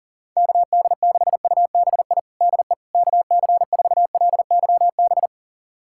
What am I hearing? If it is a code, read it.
Morse code, heard as a keyed tone, about 40 wpm, 700 Hz. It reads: KD6UBI DE KC4LYB
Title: USA Callsign DE Callsign - 6 Characters - 40wpm